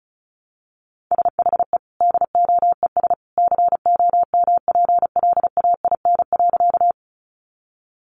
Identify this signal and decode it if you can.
Morse code, heard as a keyed tone, about 34 wpm, 700 Hz. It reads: SHE DOES COMPLAIN.
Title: Sentences from Top 1300 Words 35wpm